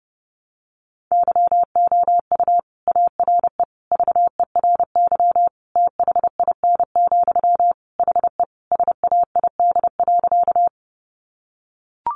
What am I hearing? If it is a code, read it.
Morse code, heard as a keyed tone, about 30 wpm, 700 Hz. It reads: YOU ARE VERY THIN, HE SAID.
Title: Sentences from Top 1200 Words 30wpm